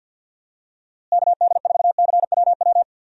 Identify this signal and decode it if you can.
Morse code, heard as a keyed tone, about 50 wpm, 700 Hz. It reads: KD4CWW
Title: USA Call Signs - 6 Characters - Repeated 3x - 50wpm